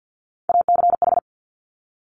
Morse code, heard as a keyed tone, about 50 wpm, 690 Hz. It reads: ACH
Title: Top Three Letter Combinations in Words 50wpm